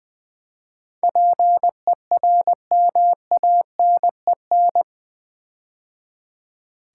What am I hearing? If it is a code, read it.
Morse code, heard as a keyed tone, about 20 wpm, 700 Hz. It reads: PERMANEN
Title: Build A Word - 4 Letters At A Time - 11-Letter Words - 20wpm